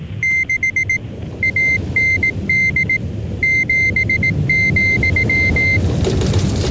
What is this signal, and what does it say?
Morse code, heard as a keyed tone, about 18 wpm, 2100 Hz. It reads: 6 AND 7,